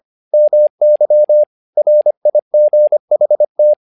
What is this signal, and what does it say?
Morse code, heard as a keyed tone, about 25 wpm, 600 Hz. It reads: MY RIGHT
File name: learning center / tower trouble / tower trouble 25wpm